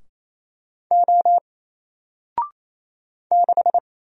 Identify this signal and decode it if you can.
Morse code, heard as a keyed tone, about 27 wpm, 700 Hz. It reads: O 6
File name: Single-Letter-Number-28wpm